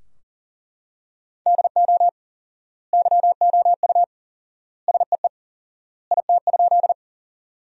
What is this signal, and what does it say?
Morse code, heard as a keyed tone, about 40 wpm, 700 Hz. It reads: DO YOU SEE IT?
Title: 4x Word-Spacing - Sentences from Top 100 Words - 40wpm